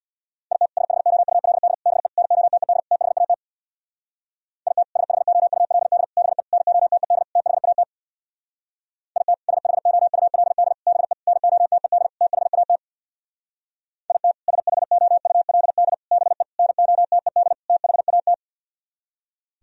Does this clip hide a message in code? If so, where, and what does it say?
Yes: Morse code from 0.5 to 18.3 s, heard as a keyed tone, about 64 wpm, 700 Hz. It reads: IT SHOULD BE NOTED THAT IT SHOULD BE NOTED THAT IT SHOULD BE NOTED THAT IT SHOULD BE NOTED THAT